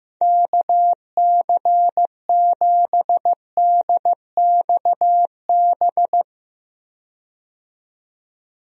Morse code, heard as a keyed tone, about 15 wpm, 700 Hz. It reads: KC7DXB
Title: USA Call Signs - 6 Characters - 15wpm